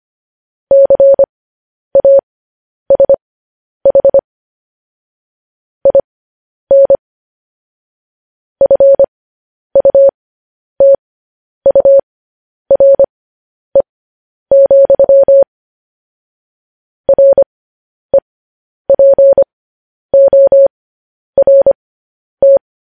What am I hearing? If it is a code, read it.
Morse code, heard as a keyed tone, about 25 wpm, spaced as for 10 wpm, 550 Hz. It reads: CASH IN FUTURE, REPORT
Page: news headlines in morse code at 10 wpm